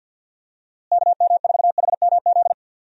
Morse code, heard as a keyed tone, about 51 wpm, 700 Hz. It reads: KM4HMZ
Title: USA Call Signs - 6 Characters - Repeated 3x - 50wpm